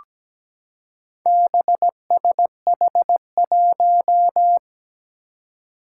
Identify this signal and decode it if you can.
Morse code, heard as a keyed tone, about 17 wpm, 700 Hz. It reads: BSH1
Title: Four Challenging Letter-Number Combinations 17wpm